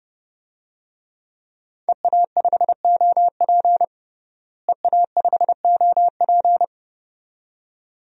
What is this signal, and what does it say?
Morse code, heard as a keyed tone, about 32 wpm, 700 Hz. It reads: EA5OP EA5OP
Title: DXCC-100 Call Signs - 5 Characters - Repeated 2x - 30wpm